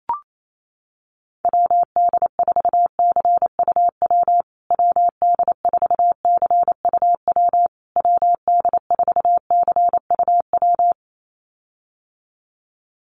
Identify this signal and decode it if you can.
Morse code, heard as a keyed tone, about 28 wpm, 700 Hz. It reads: WD4CUW WD4CUW WD4CUW
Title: USA Call Signs Repeated 3x 28wpm